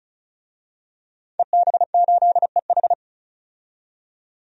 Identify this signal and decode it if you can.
Morse code, heard as a keyed tone, about 36 wpm, 700 Hz. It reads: EB8EH
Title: DXCC-100 Call Signs - 5 Characters - 35wpm